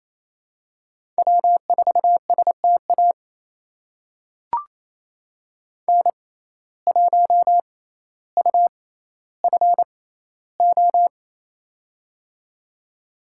Morse code, heard as a keyed tone, about 28 wpm, spaced as for 14 wpm, 700 Hz. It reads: W4STA N1UFO
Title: USA Call Signs - 5 Characters - 6x Character Spacing - 28wpm